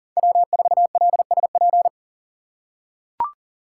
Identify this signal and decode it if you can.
Morse code, heard as a keyed tone, about 40 wpm, 700 Hz. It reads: W4LSP
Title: USA Call Signs - 5 Characters - 4x Character Spacing - 40wpm